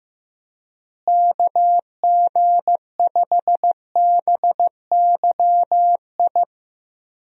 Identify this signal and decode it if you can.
Morse code, heard as a keyed tone, about 15 wpm, 700 Hz. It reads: KG5BYI
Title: USA Call Signs - 6 Characters - Repeated 3x - 15wpm